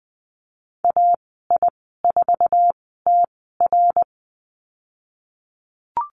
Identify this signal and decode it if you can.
Morse code, heard as a keyed tone, about 20 wpm, spaced as for 15 wpm, 700 Hz. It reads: AI4TR
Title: 055 Lesson 10g - Letter r - Call Signs 20wpm